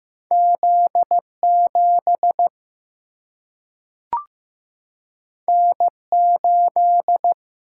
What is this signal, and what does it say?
Morse code, heard as a keyed tone, about 15 wpm, 700 Hz. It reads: Z 7 N 8
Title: Single-Letter Single-Number 15wpm